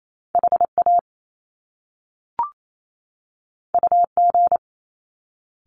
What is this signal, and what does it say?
Morse code, heard as a keyed tone, about 28 wpm, 700 Hz. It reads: HA UG
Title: Top Two Letter Combinations in Words 28wpm